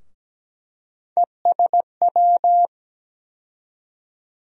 Morse code, heard as a keyed tone, about 17 wpm, 700 Hz. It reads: ESW
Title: USA Call Sign Postfixes - 3 Letters 17wpm